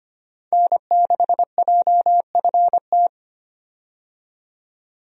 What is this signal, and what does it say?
Morse code, heard as a keyed tone, about 25 wpm, 700 Hz. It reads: N6JFT
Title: USA Call Signs - 5 Characters - 25wpm